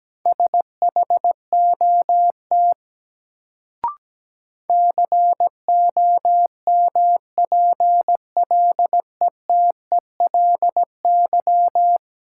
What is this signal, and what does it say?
Morse code, heard as a keyed tone, about 17 wpm, 700 Hz. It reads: SHOT COMPLETELY
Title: Sentences from Top 1100 Words - New Word Drill Alt - 17wpm